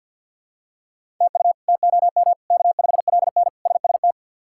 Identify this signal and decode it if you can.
Morse code, heard as a keyed tone, about 47 wpm, 700 Hz. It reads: TU TOM K5LN SST
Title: K1USN SST QSO Practice - 50wpm